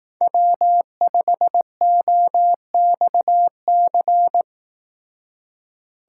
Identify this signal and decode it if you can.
Morse code, heard as a keyed tone, about 18 wpm, 700 Hz. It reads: W5OXC
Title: Speed Racing - USA 1x3 Call Signs 15wpm